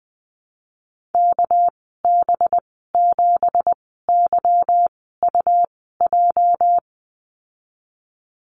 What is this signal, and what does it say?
Morse code, heard as a keyed tone, about 20 wpm, spaced as for 15 wpm, 700 Hz. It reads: KB7YUJ
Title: 231 Lesson 37g - Letter x - Call Signs 20wpm